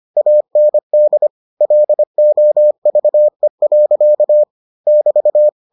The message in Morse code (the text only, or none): AND LOVE. <BT>